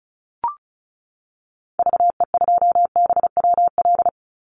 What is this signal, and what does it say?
Morse code, heard as a keyed tone, about 34 wpm, 700 Hz. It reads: VE2BWL
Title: Call Signs Repeated 3x - 35wpm